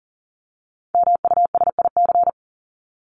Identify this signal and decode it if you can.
Morse code, heard as a keyed tone, about 40 wpm, 700 Hz. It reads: MUSIC
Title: Top 300 Words 40wpm